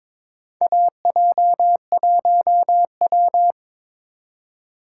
Morse code, heard as a keyed tone, about 21 wpm, 700 Hz. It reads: AJ1W
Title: USA Call Signs - 4 Characters - 8x Character Spacing - 22wpm